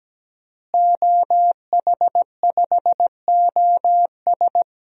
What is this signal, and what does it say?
Morse code, heard as a keyed tone, about 17 wpm, 700 Hz. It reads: OH5OS
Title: DXCC-100 Call Signs - 5 Characters - Repeated 2x - 17wpm